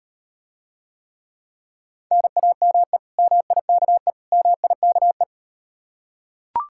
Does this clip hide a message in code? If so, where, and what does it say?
Yes: Morse code from 2.1 to 5.2 s, heard as a keyed tone, about 39 wpm, 700 Hz. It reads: NAME MIKE MIKE